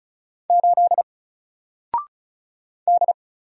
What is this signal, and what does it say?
Morse code, heard as a keyed tone, about 36 wpm, 700 Hz. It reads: 8 D
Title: Single Character (Letter:Number:Punctuation:Prosign) Alt 35wpm